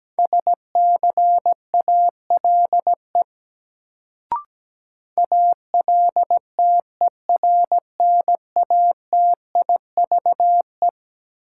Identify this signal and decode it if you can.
Morse code, heard as a keyed tone, about 17 wpm, 700 Hz. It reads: SCALE ALTERNATIVE
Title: Sentences from Top 1000 Words - Review Part 4 - New Word Drill - 17wpm